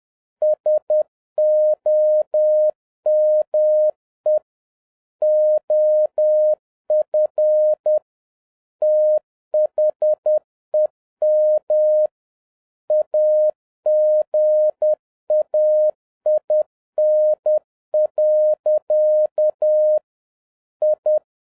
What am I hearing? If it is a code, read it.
Morse code, heard as a keyed tone, about 10 wpm, 610 Hz. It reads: SOME OF THEM AGAIN. I
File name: 2011-07-mp3 / n46 JOHN K3TN 10wpm